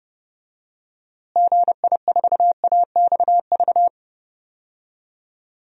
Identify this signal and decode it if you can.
Morse code, heard as a keyed tone, about 28 wpm, 700 Hz. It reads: GI4AXV
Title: DXCC-100 Call Signs - 6 Characters - Repeated 3x - 30wpm